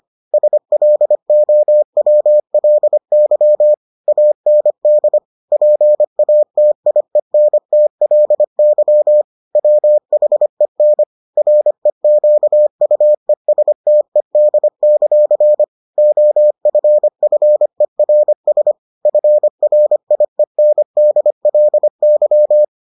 Morse code, heard as a keyed tone, about 25 wpm, 600 Hz. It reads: SLOWLY AND PATIENTLY WHEN REQUESTED; OFFERS FRIENDLY